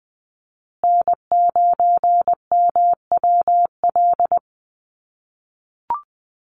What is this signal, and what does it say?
Morse code, heard as a keyed tone, about 20 wpm, 700 Hz. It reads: N9MWL
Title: USA Call Signs Repeated 2x 20wpm